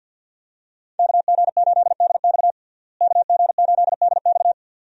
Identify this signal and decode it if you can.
Morse code, heard as a keyed tone, about 53 wpm, 700 Hz. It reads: KG8DX KG8DX
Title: USA Call Signs - 5 Characters - Repeated 2x - 50wpm